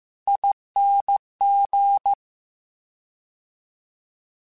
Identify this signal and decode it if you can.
Morse code, heard as a keyed tone, about 15 wpm, 800 Hz. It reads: ING
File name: Special INGs-mp3s / Special INGs-mp3s / Fast ing @13